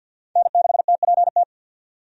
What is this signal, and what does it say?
Morse code, heard as a keyed tone, about 46 wpm, 700 Hz. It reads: N6TPT
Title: USA Call Signs - 5 Characters - 50wpm